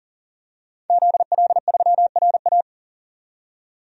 Morse code, heard as a keyed tone, about 40 wpm, 700 Hz. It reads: ZL3RA
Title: DXCC-100 Call Signs - 5 Characters - 40wpm